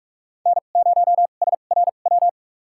Morse code, heard as a keyed tone, about 45 wpm, spaced as for 32 wpm, 700 Hz. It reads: N0SRW
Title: USA Call Signs - 5 Characters - 2x Character Spacing - 45wpm